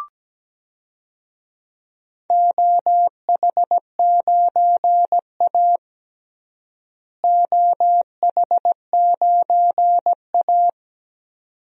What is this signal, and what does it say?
Morse code, heard as a keyed tone, about 17 wpm, 700 Hz. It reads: OH9A OH9A
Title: DXCC-100 Call Signs - 4 Characters - Repeated 2x - 17wpm